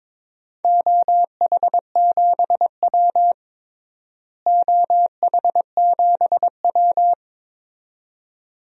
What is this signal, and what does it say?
Morse code, heard as a keyed tone, about 22 wpm, 700 Hz. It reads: OH7W OH7W
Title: DXCC-100 Call Signs - 4 Characters - Repeated 2x - 22wpm